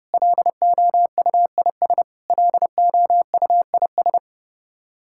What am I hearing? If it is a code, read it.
Morse code, heard as a keyed tone, about 30 wpm, 700 Hz. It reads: LOUIS LOUIS
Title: K1USN SST QSO Practice 30wpm